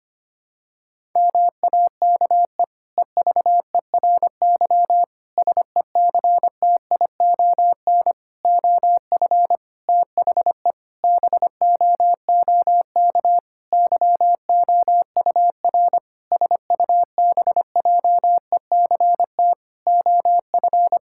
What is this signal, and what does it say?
Morse code, heard as a keyed tone, about 25 wpm, 700 Hz. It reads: MAKE EVERY SECTION OF THE BOOK YOUR SUBJECT OF
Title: Sentences from Top 1000 Words - Review Part 5 - 25wpm